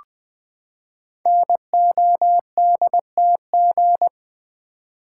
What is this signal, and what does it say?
Morse code, heard as a keyed tone, about 20 wpm, 700 Hz. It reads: NODTG
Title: Code Groups - 5 Letters - 20wpm